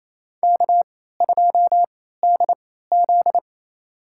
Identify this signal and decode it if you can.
Morse code, heard as a keyed tone, about 28 wpm, spaced as for 16 wpm, 700 Hz. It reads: K2DZ
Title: USA Call Signs - 3x Character Spacing - 28wpm